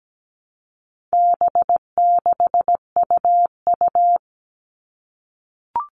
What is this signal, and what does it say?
Morse code, heard as a keyed tone, about 17 wpm, 700 Hz. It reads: B6UU
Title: All Cut Numbers - 4-Digit 17wpm